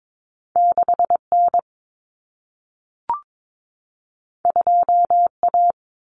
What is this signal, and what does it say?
Morse code, heard as a keyed tone, about 22 wpm, 700 Hz. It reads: 6N 2A
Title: Common Cut Numbers - 2-Digit 22wpm